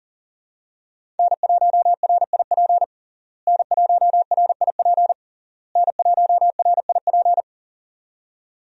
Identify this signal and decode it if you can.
Morse code, heard as a keyed tone, about 40 wpm, 700 Hz. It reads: N1RIP N1RIP N1RIP
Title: USA Call Signs - 5 Characters - Repeated 3x - 40wpm